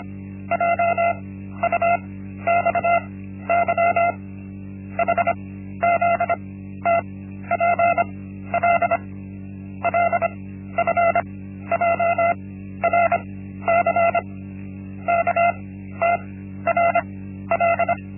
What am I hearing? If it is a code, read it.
Morse code, heard as a keyed tone, about 25 wpm, 680 Hz. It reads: J U X Y H Z T P L L F J R C K T R L
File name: Linksmakalnis-1